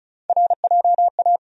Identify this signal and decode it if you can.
Morse code, heard as a keyed tone, about 36 wpm, 700 Hz. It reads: RJA
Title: USA Call Sign Postfixes - 3 Letters 35wpm